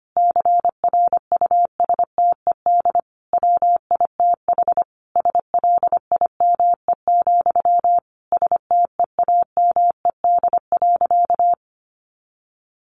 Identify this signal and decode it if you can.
Morse code, heard as a keyed tone, about 25 wpm, 700 Hz. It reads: CRUSTED WITH SLIME, STEAMED.
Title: A Sound Of Thunder 25wpm